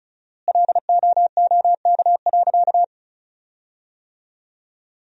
Morse code, heard as a keyed tone, about 36 wpm, 700 Hz. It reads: LOOK.
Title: 4x Word-Spacing - Sentences from Top 300 Words - 35wpm